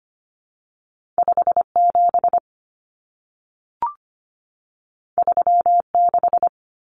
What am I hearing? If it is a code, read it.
Morse code, heard as a keyed tone, about 25 wpm, 700 Hz. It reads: 57 36
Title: Common Cut Numbers - 2-Digit 25wpm